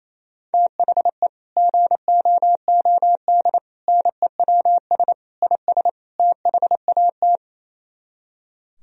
Morse code, heard as a keyed tone, about 38 wpm, 700 Hz. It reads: THE GOOD NEWS IS THAT